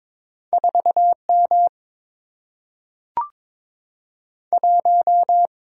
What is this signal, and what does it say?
Morse code, heard as a keyed tone, about 22 wpm, 700 Hz. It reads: 4M 1
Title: Single-Letter Single-Number 22wpm